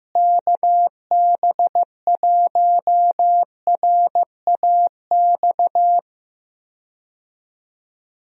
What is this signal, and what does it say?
Morse code, heard as a keyed tone, about 15 wpm, 700 Hz. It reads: KB1RAX
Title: USA Call Signs - 6 Characters - Repeated 2x - 15wpm